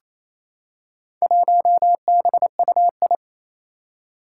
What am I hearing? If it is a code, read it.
Morse code, heard as a keyed tone, about 28 wpm, 700 Hz. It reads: 1BUI
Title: Four Challenging Letter-Number Combinations 28wpm